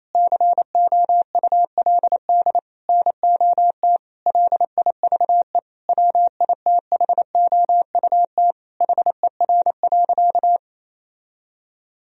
Morse code, heard as a keyed tone, about 28 wpm, 700 Hz. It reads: COULD NOT LIVE WITHOUT HER.
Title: Sentences from Top 1000 Words - Review Part 3 - 28wpm